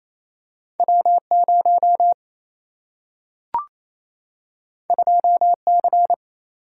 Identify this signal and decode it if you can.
Morse code, heard as a keyed tone, about 28 wpm, 700 Hz. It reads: W0 2C
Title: Single-Letter Single-Number 28wpm